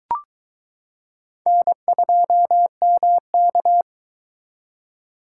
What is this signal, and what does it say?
Morse code, heard as a keyed tone, about 22 wpm, 700 Hz. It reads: N2MK